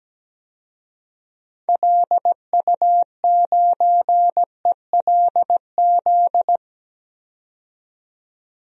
Morse code, heard as a keyed tone, about 17 wpm, 700 Hz. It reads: LU9ELZ
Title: DXCC-100 Call Signs - 6 Characters - 17wpm